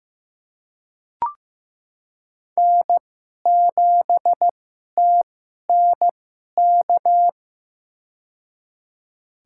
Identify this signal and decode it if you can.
Morse code, heard as a keyed tone, about 16 wpm, spaced as for 11 wpm, 700 Hz. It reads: N7TNK
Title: USA Call Signs - 2x Character Spacing - 15wpm